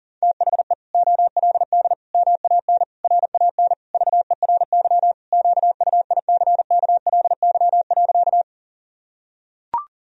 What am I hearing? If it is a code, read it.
Morse code, heard as a keyed tone, about 37 wpm, 700 Hz. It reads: THE OLD MAN RAN VERY QUICKLY.
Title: Sentences from Top 1000 Words - Review Part 3 - 40wpm